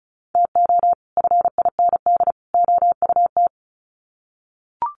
Morse code, heard as a keyed tone, about 34 wpm, 700 Hz. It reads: TO FIND OUT
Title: Sets of 3 Words 35wpm